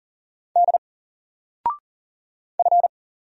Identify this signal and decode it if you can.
Morse code, heard as a keyed tone, about 40 wpm, 700 Hz. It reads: D F
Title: Single Character (Letter:Number:Punctuation:Prosign) 40wpm